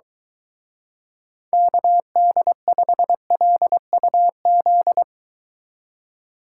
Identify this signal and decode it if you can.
Morse code, heard as a keyed tone, about 23 wpm, 700 Hz. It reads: KD5LUZ